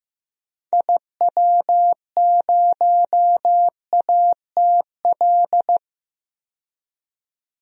Morse code, heard as a keyed tone, about 15 wpm, 700 Hz. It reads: IW0ATL
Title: DXCC-100 Call Signs - 6 Characters - 15wpm